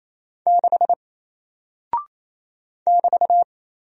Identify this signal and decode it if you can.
Morse code, heard as a keyed tone, about 28 wpm, 700 Hz. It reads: 6 <BT>
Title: Single Character (Letter:Number:Punctuation:Prosign) Alt 28wpm